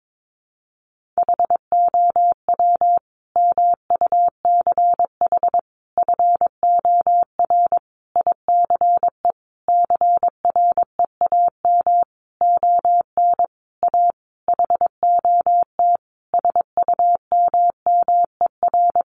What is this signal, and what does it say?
Morse code, heard as a keyed tone, about 22 wpm, 700 Hz. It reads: HOW MUCH FOR ICE CREAM ON A HOT SUMMER
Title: Sentences from Top 1500 Words 22wpm